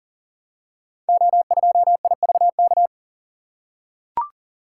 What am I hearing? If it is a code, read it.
Morse code, heard as a keyed tone, about 40 wpm, 700 Hz. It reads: O2IVK
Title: Code Groups - 5 Letter:Numbers - 40wpm